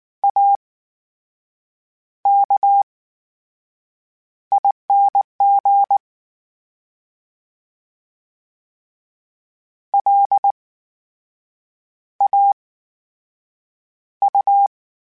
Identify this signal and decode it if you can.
Morse code, heard as a keyed tone, about 19 wpm, spaced as for 6 wpm, 800 Hz. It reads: AKING LAU
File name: Special INGs / Special INGs-mp3s / VerySlowToFaster Ing @18